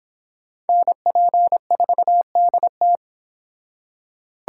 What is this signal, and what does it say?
Morse code, heard as a keyed tone, about 30 wpm, 700 Hz. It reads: NP4DT